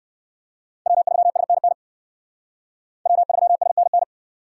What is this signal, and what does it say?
Morse code, heard as a keyed tone, about 69 wpm, 700 Hz. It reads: W3SNN W3SNN